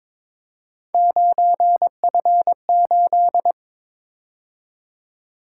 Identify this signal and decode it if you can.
Morse code, heard as a keyed tone, about 24 wpm, 700 Hz. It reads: 9F8
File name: Speed-Racing - Three Letter-Number Combinations - 22wpm